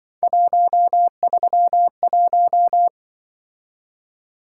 Morse code, heard as a keyed tone, about 24 wpm, 700 Hz. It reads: 131
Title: Speed-Racing - Three Letter-Number Combinations - 20wpm